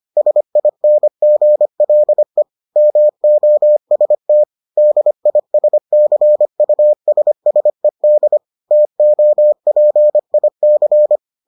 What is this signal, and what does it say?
Morse code, heard as a keyed tone, about 25 wpm, 600 Hz. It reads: SINGLE MOST DISCUSSED TOPIC